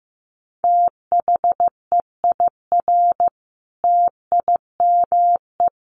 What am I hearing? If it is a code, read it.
Morse code, heard as a keyed tone, about 15 wpm, 700 Hz. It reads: THEIR TIME
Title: Sentences from Top 1300 Words 15wpm